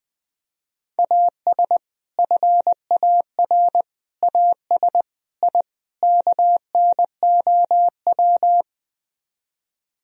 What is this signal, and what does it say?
Morse code, heard as a keyed tone, about 20 wpm, 700 Hz. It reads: AS FAR AS I KNOW
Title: Speed-Racing - Sets of 5 Words 20wpm